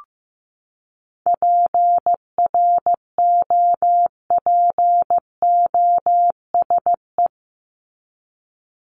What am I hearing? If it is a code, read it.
Morse code, heard as a keyed tone, about 15 wpm, 700 Hz. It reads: PROPOSE
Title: ED Suffix - Top 100 Words with Base Words 15wpm